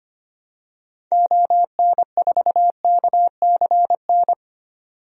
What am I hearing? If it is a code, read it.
Morse code, heard as a keyed tone, about 25 wpm, 700 Hz. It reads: ON4KCN
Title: DXCC-100 Call Signs - 6 Characters - 25wpm